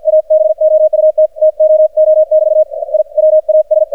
Morse code, heard as a keyed tone, about 50 wpm, spaced as for 44 wpm, 620 Hz. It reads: ABOUT TOO <BT> I WAS
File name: archived / CRSnet